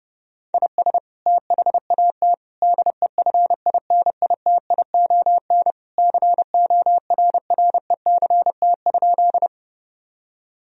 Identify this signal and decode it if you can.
Morse code, heard as a keyed tone, about 30 wpm, 700 Hz. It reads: IS THAT DEFINITION CORRECT?
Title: Sentences from Top 1000 Words - Review Part 4 - 30wpm